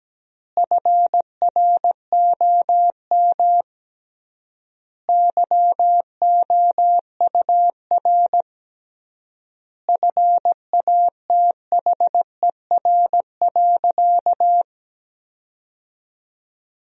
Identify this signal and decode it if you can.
Morse code, heard as a keyed tone, about 17 wpm, 700 Hz. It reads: FROM YOUR FATHER.
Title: 3x Word-Spacing - Sentences from Top 300 Words - 17wpm